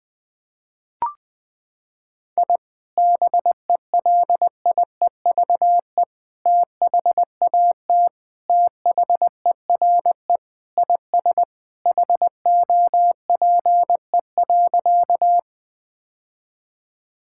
Morse code, heard as a keyed tone, about 20 wpm, 700 Hz. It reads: I BELIEVE THAT THERE IS HOPE.
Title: Sentences with 3-Word Phrases - 20wpm